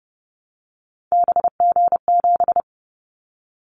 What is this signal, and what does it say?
Morse code, heard as a keyed tone, about 30 wpm, 700 Hz. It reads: BG7
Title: Callsign Prefixes for Top 100 DXCC - 30wpm